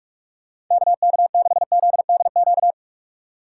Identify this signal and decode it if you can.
Morse code, heard as a keyed tone, about 46 wpm, 700 Hz. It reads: KK6ZDQ
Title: USA Call Signs - 6 Characters - 6x Character Spacing - 45wpm